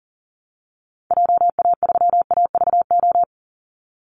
Morse code, heard as a keyed tone, about 40 wpm, 700 Hz. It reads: JA3AVO
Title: Call Signs Repeated 3x - 40wpm